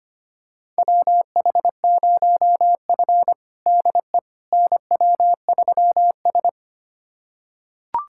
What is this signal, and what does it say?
Morse code, heard as a keyed tone, about 25 wpm, 700 Hz. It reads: WH0F DE NW3S
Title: USA Callsign DE Callsign - 4 Characters - 25wpm